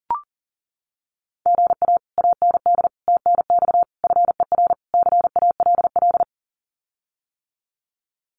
Morse code, heard as a keyed tone, about 40 wpm, 700 Hz. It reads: GA AND TNX FER CALL
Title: Common QSO Phrases Encore 40wpm